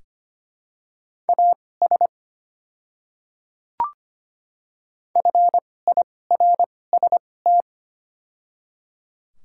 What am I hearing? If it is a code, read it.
Morse code, heard as a keyed tone, about 25 wpm, spaced as for 18 wpm, 700 Hz. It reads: AS FIRST
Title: 2x Character Spacing - Top 100 Words - 25wpm